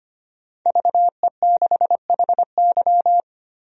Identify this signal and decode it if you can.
Morse code, heard as a keyed tone, about 25 wpm, 700 Hz. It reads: VE6HY
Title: DXCC-100 Call Signs - 5 Characters - 25wpm